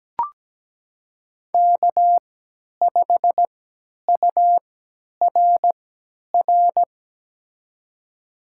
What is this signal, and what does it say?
Morse code, heard as a keyed tone, about 17 wpm, spaced as for 10 wpm, 700 Hz. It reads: K5URR
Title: USA Call Signs - 5 Characters - 3x Character Spacing - 17wpm